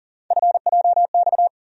Morse code, heard as a keyed tone, about 40 wpm, 700 Hz. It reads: FJX
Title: USA Call Sign Postfixes - 3 Letters 40wpm